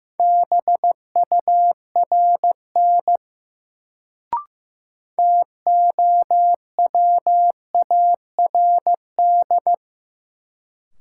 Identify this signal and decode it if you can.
Morse code, heard as a keyed tone, about 15 wpm, 700 Hz. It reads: BURN TOWARD
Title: Sentences from Top 1300 Words - New Word Drill Alt - 15wpm